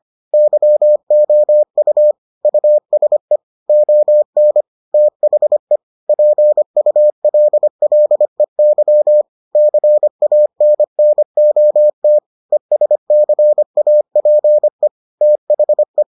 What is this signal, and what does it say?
Morse code, heard as a keyed tone, about 25 wpm, 600 Hz. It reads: YOU USE ON THE PULLEY CANNOT ESCAPE THE